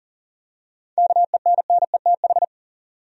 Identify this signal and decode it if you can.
Morse code, heard as a keyed tone, about 40 wpm, 700 Hz. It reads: KENNETH
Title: USA Common Names - 40wpm